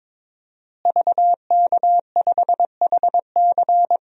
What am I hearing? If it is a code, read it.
Morse code, heard as a keyed tone, about 22 wpm, 700 Hz. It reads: VK5HC
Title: DXCC-100 Call Signs - 5 Characters - Repeated 3x - 22wpm